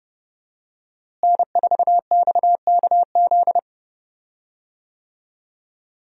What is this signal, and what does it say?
Morse code, heard as a keyed tone, about 32 wpm, 700 Hz. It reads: N4XKZ
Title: USA Call Signs - 5 Characters - Repeated 2x - 30wpm